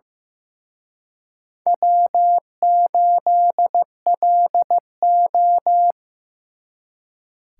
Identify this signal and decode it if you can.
Morse code, heard as a keyed tone, about 18 wpm, 700 Hz. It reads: W8LO
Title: Speed Racing - USA Call Signs 15wpm